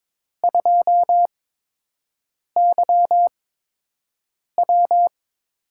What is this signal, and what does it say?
Morse code, heard as a keyed tone, about 22 wpm, spaced as for 6 wpm, 700 Hz. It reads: 2YW
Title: USA Call Signs - 4 Characters - 8x Character Spacing - 22wpm